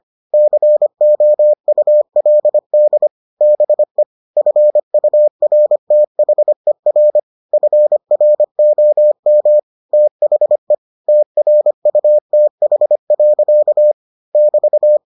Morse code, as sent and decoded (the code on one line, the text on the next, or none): -.-. --- ..- .-.. -.. / -... / ..-. ..- .-. - .... .-. / ..-. .-. --- -- / - .... / - .-. ..- - .... .-.-.- / -...-
COULD BE FURTHER FROM THE TRUTH. <BT>